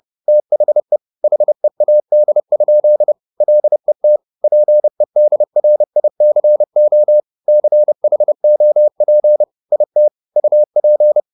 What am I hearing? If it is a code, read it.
Morse code, heard as a keyed tone, about 30 wpm, 600 Hz. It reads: THE HEAD? LET PEDRICO CHOP IT UP